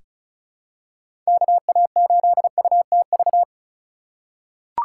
Morse code, heard as a keyed tone, about 36 wpm, 700 Hz. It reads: KA8UTV
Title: USA Call Signs - 2x Character Spacing - 35wpm